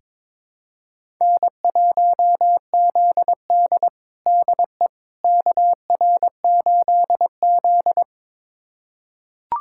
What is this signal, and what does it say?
Morse code, heard as a keyed tone, about 22 wpm, 700 Hz. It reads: N1ZD DE KR8Z
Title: USA Callsign DE Callsign - 4 Characters - 22wpm